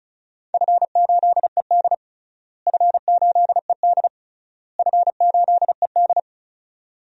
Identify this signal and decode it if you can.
Morse code, heard as a keyed tone, about 36 wpm, 700 Hz. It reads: F8ED F8ED F8ED
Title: DXCC-100 Call Signs - 4 Characters - Repeated 3x - 35wpm